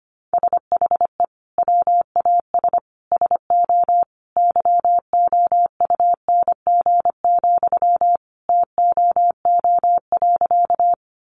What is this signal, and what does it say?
Morse code, heard as a keyed tone, about 25 wpm, 700 Hz. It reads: SHE WAS SO YOUNG, TOO.
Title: Sentences from Top 400 Words 25wpm